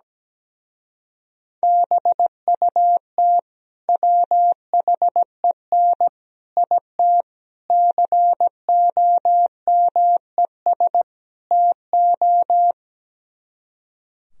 Morse code, heard as a keyed tone, about 17 wpm, 700 Hz. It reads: BUT WHEN IT COMES TO
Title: Speed-Racing - Sets of 5 Words 17wpm